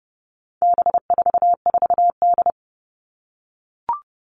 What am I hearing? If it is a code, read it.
Morse code, heard as a keyed tone, about 30 wpm, 700 Hz. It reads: B44D
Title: All Cut Numbers - 4-Digit 30wpm